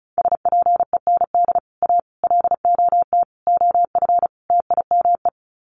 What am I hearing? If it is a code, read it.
Morse code, heard as a keyed tone, about 34 wpm, 700 Hz. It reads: SPEND A LOT OF TIME
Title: Sets of 5 Words 35wpm